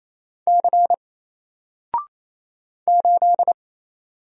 Morse code, heard as a keyed tone, about 27 wpm, 700 Hz. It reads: C 8
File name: Single-Letter-Number-28wpm